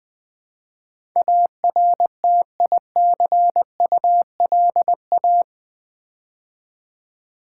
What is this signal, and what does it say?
Morse code, heard as a keyed tone, about 20 wpm, 700 Hz. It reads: ARTICULA
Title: Build A Word - 4 Letters At A Time - 10-Letter Words - 20wpm